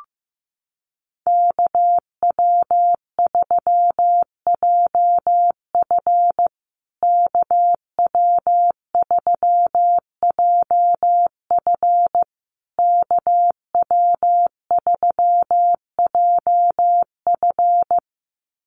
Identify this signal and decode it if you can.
Morse code, heard as a keyed tone, about 15 wpm, 700 Hz. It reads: KW3JF KW3JF KW3JF
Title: USA Call Signs Repeated 3x 15wpm